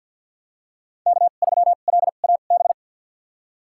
Morse code, heard as a keyed tone, about 46 wpm, spaced as for 35 wpm, 700 Hz. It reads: K3LAB
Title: USA Call Signs - 5 Characters - 2x Character Spacing - 50wpm